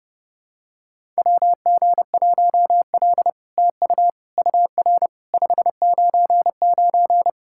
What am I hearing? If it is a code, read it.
Morse code, heard as a keyed tone, about 30 wpm, 700 Hz. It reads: WG1L TU UR 599
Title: SOTA QSO Practice 30wpm